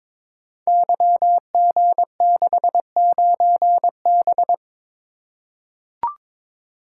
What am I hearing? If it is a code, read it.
Morse code, heard as a keyed tone, about 22 wpm, 700 Hz. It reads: YG69B
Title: Code Groups - 5 Letter:Numbers - 22wpm